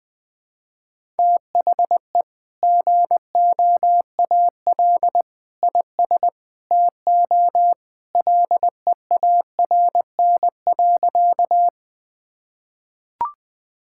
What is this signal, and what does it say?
Morse code, heard as a keyed tone, about 20 wpm, 700 Hz. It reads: THE GOAL IS TO LEARN.
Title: Sentences with 2-Word Phrases - 20wpm